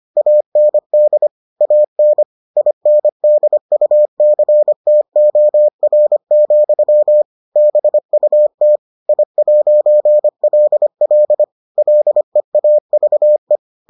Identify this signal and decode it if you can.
Morse code, heard as a keyed tone, about 25 wpm, 600 Hz. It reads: AND AN INDUCTOR, BUT I'LL LEAVE